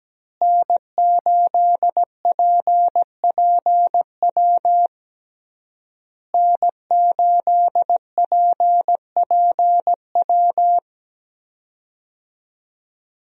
Morse code, heard as a keyed tone, about 17 wpm, 700 Hz. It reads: N8PPW N8PPW
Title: USA Call Signs - 5 Characters - Repeated 3x - 17wpm